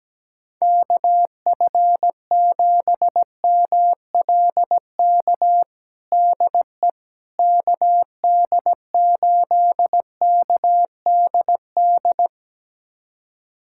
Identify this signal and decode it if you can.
Morse code, heard as a keyed tone, about 17 wpm, 700 Hz. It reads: KF7MLK DE KD8KDD
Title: USA Callsign DE Callsign 17wpm